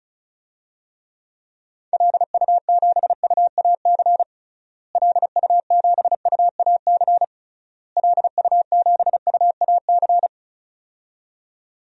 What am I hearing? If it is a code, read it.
Morse code, heard as a keyed tone, about 36 wpm, 700 Hz. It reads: LU7UAC LU7UAC LU7UAC
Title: DXCC-100 Call Signs - 6 Characters - Repeated 3x - 35wpm